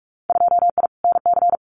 Morse code, heard as a keyed tone, about 46 wpm, 700 Hz. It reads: 2I NC
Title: Winter Field Day Exchanges 45wpm